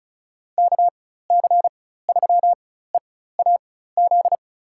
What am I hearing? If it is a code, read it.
Morse code, heard as a keyed tone, about 36 wpm, spaced as for 16 wpm, 700 Hz. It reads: KC3EAZ
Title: USA Call Signs - 6 Characters - 4x Character Spacing - 35wpm